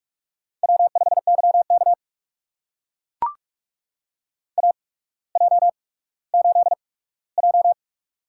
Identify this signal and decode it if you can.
Morse code, heard as a keyed tone, about 44 wpm, spaced as for 19 wpm, 690 Hz. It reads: W5YK AJ8J
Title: USA Call Signs - 4 Characters - 8x Character Spacing - 45wpm